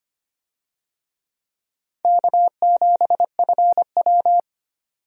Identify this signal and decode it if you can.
Morse code, heard as a keyed tone, about 25 wpm, 700 Hz. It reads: K7FW
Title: USA Call Signs - 4 Characters - 25wpm